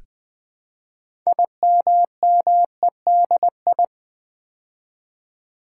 Morse code, heard as a keyed tone, about 20 wpm, 700 Hz. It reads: IMMEDI
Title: Build A Word - 3 Letters At A Time - 9-Letter Words - 20wpm